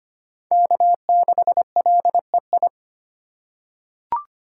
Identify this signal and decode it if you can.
Morse code, heard as a keyed tone, about 25 wpm, 700 Hz. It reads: K6LEI
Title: USA Call Signs - 5 Characters - 4x Character Spacing - 25wpm